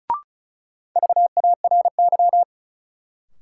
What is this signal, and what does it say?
Morse code, heard as a keyed tone, about 34 wpm, 700 Hz. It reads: VARY
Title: Sentences from Top 1200 Words - New Word Drill - 35wpm